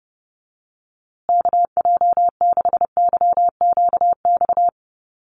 Morse code, heard as a keyed tone, about 28 wpm, 700 Hz. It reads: KJ6YQX
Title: USA Call Signs Repeated 3x 30wpm